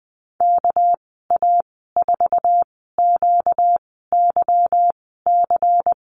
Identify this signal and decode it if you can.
Morse code, heard as a keyed tone, about 20 wpm, spaced as for 15 wpm, 700 Hz. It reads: KA4QYC
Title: 245 Lesson 39g - Letter z - Call Signs 20wpm